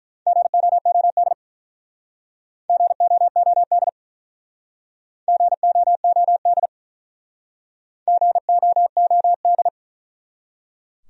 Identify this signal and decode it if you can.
Morse code, heard as a keyed tone, about 48 wpm, 700 Hz. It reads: GOOD GOOD GOOD GOOD